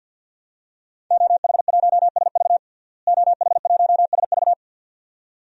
Morse code, heard as a keyed tone, about 53 wpm, 700 Hz. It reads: OH1SV OH1SV
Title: DXCC-100 Call Signs - 5 Characters - Repeated 2x - 50wpm